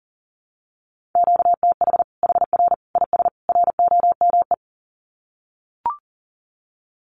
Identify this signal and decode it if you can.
Morse code, heard as a keyed tone, about 40 wpm, 700 Hz. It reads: QTH HR IS ROME
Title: Common QSO Phrases 40wpm